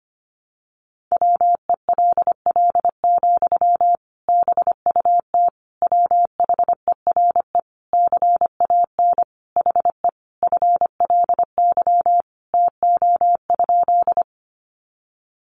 Morse code, heard as a keyed tone, about 25 wpm, 700 Hz. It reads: WELL, BUT WHERE CAN HE FLY TO?
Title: Sentences from Top 500 Words 25wpm